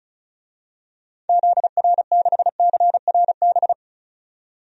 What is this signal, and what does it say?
Morse code, heard as a keyed tone, about 36 wpm, 700 Hz. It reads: ZR6CRB
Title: DXCC-100 Call Signs - 6 Characters - 35wpm